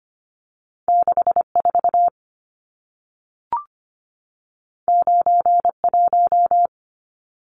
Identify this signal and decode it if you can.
Morse code, heard as a keyed tone, about 25 wpm, 700 Hz. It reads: 64 91
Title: Numbers - 2-Digit 25wpm